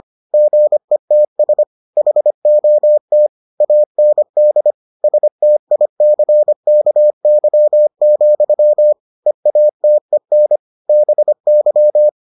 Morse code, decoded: GETS HOT AND STICKY, EATEN BY

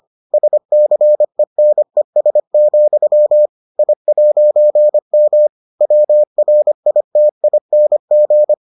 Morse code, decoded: SCENES, I'M WRITING

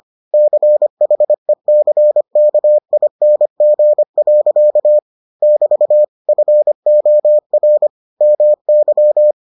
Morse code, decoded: CHECKING. <BT> FOR MY

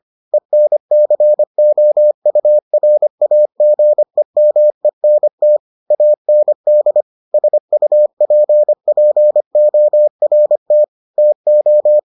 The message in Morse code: ENCOURAGEMENT AND SUPPORT TO

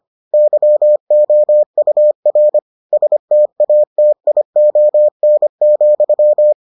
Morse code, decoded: YOUR STATION,

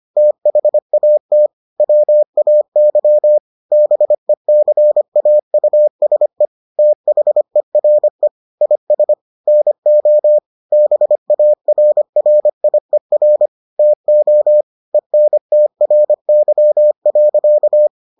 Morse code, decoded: THAT WAY BECAUSE THERE IS NO BARRIER TO ENTRY.

